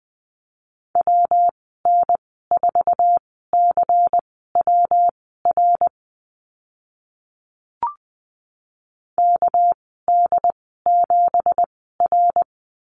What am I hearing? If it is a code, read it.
Morse code, heard as a keyed tone, about 20 wpm, spaced as for 15 wpm, 700 Hz. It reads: WN4CWR KD7R
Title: 207 Lesson 33g - Letter j - Call Signs 20wpm